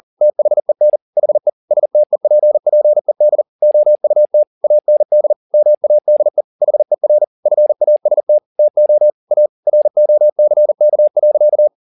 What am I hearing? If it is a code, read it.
Morse code, heard as a keyed tone, about 40 wpm, 600 Hz. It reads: THEN HE STEPPED OUT AND MADE HER FAST TO A ROCK.
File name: the old man and the sea 2 40wpm